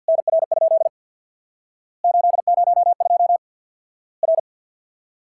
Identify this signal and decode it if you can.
Morse code, heard as a keyed tone, about 50 wpm, 700 Hz. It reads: NR? 802 R
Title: Worked All Europe DX Contest - QTC Exchange with Repeats - 50wpm